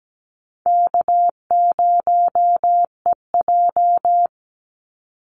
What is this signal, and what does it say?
Morse code, heard as a keyed tone, about 17 wpm, 700 Hz. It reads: K0EJ
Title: Call Signs - 17wpm